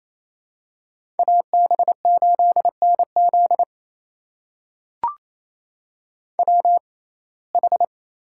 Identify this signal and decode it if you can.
Morse code, heard as a keyed tone, about 28 wpm, spaced as for 20 wpm, 700 Hz. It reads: AB8NZ WH
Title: USA Call Signs - 5 Characters - 6x Character Spacing - 28wpm